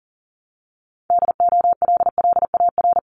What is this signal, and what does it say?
Morse code, heard as a keyed tone, about 42 wpm, 700 Hz. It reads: DOLLAR
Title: Multi-letter Phonemes 40wpm